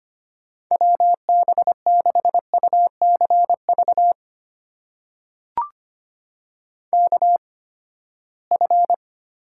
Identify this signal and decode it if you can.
Morse code, heard as a keyed tone, about 25 wpm, spaced as for 17 wpm, 700 Hz. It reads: WB6UCV KF
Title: USA Call Signs - 6 Characters - 8x Character Spacing - 25wpm